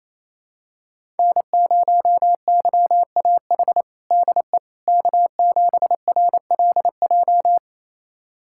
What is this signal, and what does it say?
Morse code, heard as a keyed tone, about 28 wpm, 700 Hz. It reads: N0YAH DE K7RLJ
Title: USA Callsign DE Callsign - 5 Characters - 28wpm